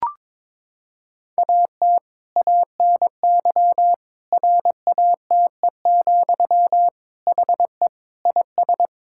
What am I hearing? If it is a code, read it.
Morse code, heard as a keyed tone, about 22 wpm, 700 Hz. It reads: AT ANY RATE, HE IS
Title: Sentences from Top 1000 Words - Review Part 5 - 22wpm